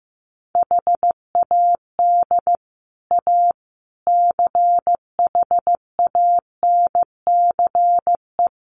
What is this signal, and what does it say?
Morse code, heard as a keyed tone, about 15 wpm, 700 Hz. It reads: HAD A CHANCE
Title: Sets of 4 Words 15wpm